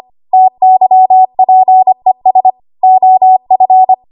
Morse code, heard as a keyed tone, about 25 wpm, 750 Hz. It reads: TYPES OF
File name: Archive / 25 WPM / 25WPM